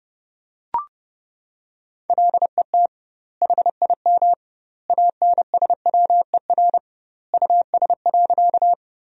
Morse code, heard as a keyed tone, about 32 wpm, 700 Hz. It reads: LET HIM ANSWER US.
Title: 2x Word-Spacing - Sentences from Top 300 Words - 30wpm